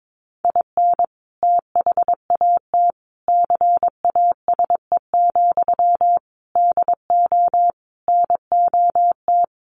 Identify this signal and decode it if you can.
Morse code, heard as a keyed tone, about 22 wpm, 700 Hz. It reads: IN THAT CASE, DO NOT
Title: Sentences from Top 1300 Words 22wpm